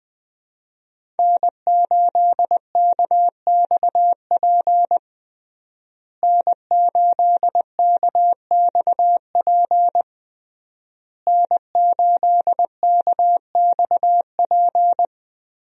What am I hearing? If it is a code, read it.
Morse code, heard as a keyed tone, about 20 wpm, 700 Hz. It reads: N8KXP N8KXP N8KXP
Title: USA Call Signs - 5 Characters - Repeated 3x - 20wpm